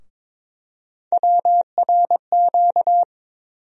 Morse code, heard as a keyed tone, about 22 wpm, 700 Hz. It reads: WRQ
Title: USA Call Sign Postfixes - 3 Letters 22wpm